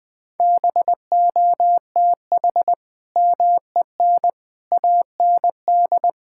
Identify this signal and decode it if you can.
Morse code, heard as a keyed tone, about 20 wpm, 700 Hz. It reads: BOTH MEN AND
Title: Sentences with 3-Word Phrases - 20wpm